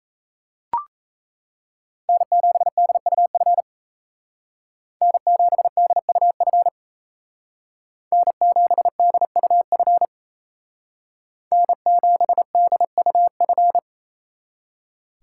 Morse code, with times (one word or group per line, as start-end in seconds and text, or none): 2.1-3.6 N7DUF
5.0-6.7 N7DUF
8.1-10.1 N7DUF
11.5-13.8 N7DUF